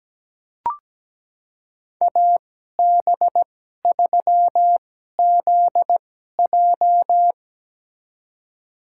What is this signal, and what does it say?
Morse code, heard as a keyed tone, about 17 wpm, spaced as for 12 wpm, 700 Hz. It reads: AB3ZJ
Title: USA Call Signs - 5 Characters - 2x Character Spacing - 17wpm